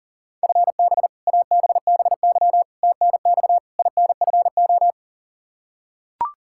Morse code, heard as a keyed tone, about 37 wpm, 700 Hz. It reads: FB ABBY TNX INFO
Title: Practice QSO - Exchange 3 - More Info - 40wpm